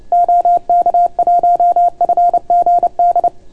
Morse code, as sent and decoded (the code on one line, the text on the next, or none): --- -.- .---- ..-. --. -..
OK1FGD